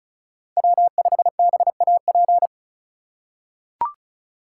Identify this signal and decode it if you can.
Morse code, heard as a keyed tone, about 36 wpm, 700 Hz. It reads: W5BAP
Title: USA Call Signs - 5 Characters - 4x Character Spacing - 35wpm